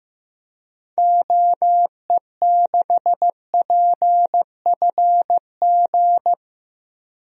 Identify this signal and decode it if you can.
Morse code, heard as a keyed tone, about 16 wpm, 700 Hz. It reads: OE6PFG
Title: DXCC-100 Call Signs - 6 Characters - 15wpm